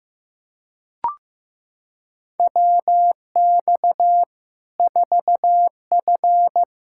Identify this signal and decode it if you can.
Morse code, heard as a keyed tone, about 15 wpm, 700 Hz. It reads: WX 4F
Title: Practice QSO - Exchange 2 - Info - 15wpm